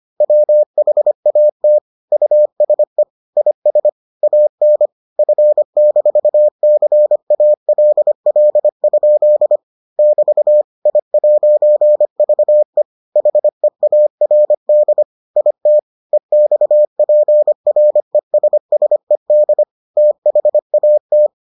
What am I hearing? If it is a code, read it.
Morse code, heard as a keyed tone, about 25 wpm, 600 Hz. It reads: WHAT USE IS AN F-CALL? <BT> I'VE HEARD IT EXPRESSED THAT